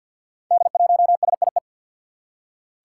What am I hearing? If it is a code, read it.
Morse code, heard as a keyed tone, about 48 wpm, 700 Hz. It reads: D1SIE
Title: Five Challenging Letter-Number Combinations 50wpm